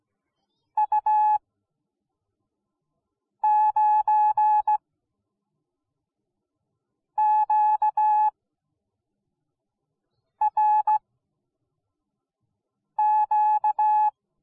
Morse code, heard as a keyed tone, about 17 wpm, spaced as for 4 wpm, 850 Hz. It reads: U9QRQ